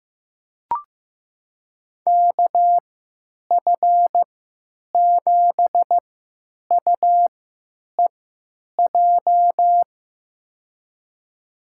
Morse code, heard as a keyed tone, about 15 wpm, spaced as for 9 wpm, 700 Hz. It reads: KF7UEJ
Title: USA Call Signs - 6 Characters - 3x Character Spacing - 15wpm